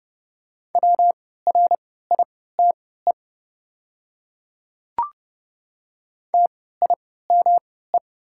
Morse code, heard as a keyed tone, about 30 wpm, spaced as for 17 wpm, 690 Hz. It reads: WRITE TIME
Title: 3x Character Spacing - Top 200 Words - 30wpm